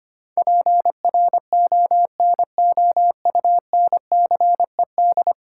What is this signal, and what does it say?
Morse code, heard as a keyed tone, about 25 wpm, 700 Hz. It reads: PRONOUNCED
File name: Ten Letter Words - 25wpm